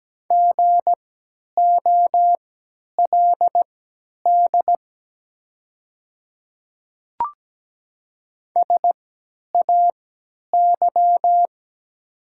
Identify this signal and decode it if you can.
Morse code, heard as a keyed tone, about 17 wpm, spaced as for 10 wpm, 700 Hz. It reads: GOLD SAY
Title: 3x Character Spacing - Top 500 Words - 17wpm